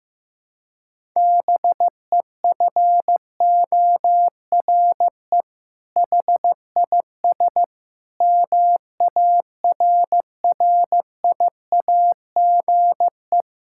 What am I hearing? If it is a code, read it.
Morse code, heard as a keyed tone, about 16 wpm, 700 Hz. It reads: BEFORE HIS MARRIAGE
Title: Sentences from Top 1000 Words - Review Part 5 - 15wpm